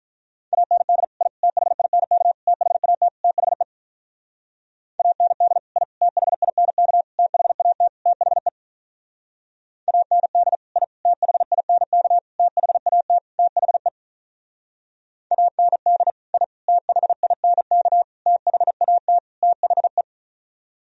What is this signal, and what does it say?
Morse code, heard as a keyed tone, about 43 wpm, 700 Hz. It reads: AND I THINK THAT THE AND I THINK THAT THE AND I THINK THAT THE AND I THINK THAT THE
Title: Speed-Racing - Sets of 5 Words 35wpm